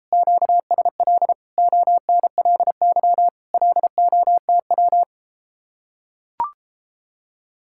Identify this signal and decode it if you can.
Morse code, heard as a keyed tone, about 33 wpm, 700 Hz. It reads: QSL ONLY LOTW